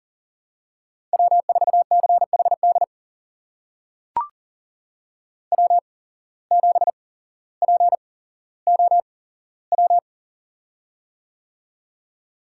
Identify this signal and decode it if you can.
Morse code, heard as a keyed tone, about 38 wpm, spaced as for 17 wpm, 700 Hz. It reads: W4CHD W7POW
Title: USA Call Signs - 5 Characters - 8x Character Spacing - 40wpm